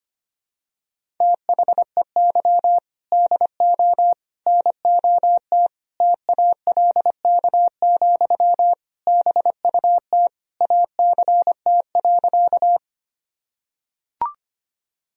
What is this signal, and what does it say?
Morse code, heard as a keyed tone, about 25 wpm, 700 Hz. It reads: THEY DO NOT TALK, BUT ACT.
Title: Sentences from Top 1000 Words - Review Part 1 - 25wpm